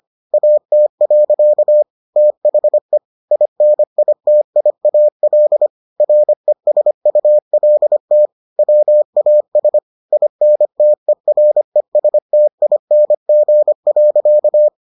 Morse code, heard as a keyed tone, about 25 wpm, 600 Hz. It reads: AT. THE INITIAL RESULT WAS INTERESTING.